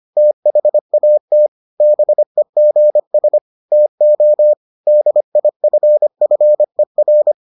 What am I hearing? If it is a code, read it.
Morse code, heard as a keyed tone, about 25 wpm, 600 Hz. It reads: THAT BEGS TO DIFFER